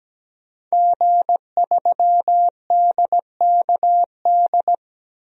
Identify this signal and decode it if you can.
Morse code, heard as a keyed tone, about 17 wpm, 700 Hz. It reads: G3DKD
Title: DXCC-100 Call Signs - 5 Characters - 17wpm